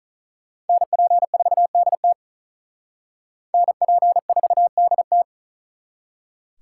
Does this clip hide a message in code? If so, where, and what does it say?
Yes: Morse code from 0.7 to 5.2 s, heard as a keyed tone, about 44 wpm, 700 Hz. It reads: NP4DT NP4DT